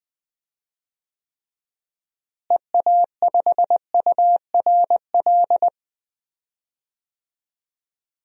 Morse code, heard as a keyed tone, about 20 wpm, 700 Hz. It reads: EA5URL
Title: DXCC-100 Call Signs - 6 Characters - Repeated 2x - 20wpm